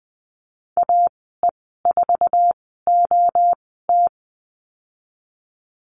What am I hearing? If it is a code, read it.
Morse code, heard as a keyed tone, about 20 wpm, spaced as for 15 wpm, 700 Hz. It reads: AE4OT
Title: 055 Lesson 10g - Letter r - Call Signs 20wpm